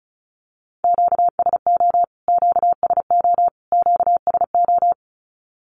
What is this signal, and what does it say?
Morse code, heard as a keyed tone, about 34 wpm, 700 Hz. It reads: QSO QSO QSO
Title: Q-Codes Repeated 3x 35wpm